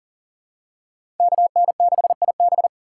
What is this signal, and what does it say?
Morse code, heard as a keyed tone, about 38 wpm, 700 Hz. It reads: KN6IB
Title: USA Call Signs - 5 Characters - 6x Character Spacing - 40wpm